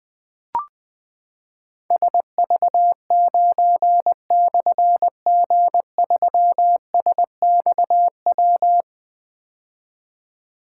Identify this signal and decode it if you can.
Morse code, heard as a keyed tone, about 20 wpm, 700 Hz. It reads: SV9/G3SXW
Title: Difficult Call Signs 20wpm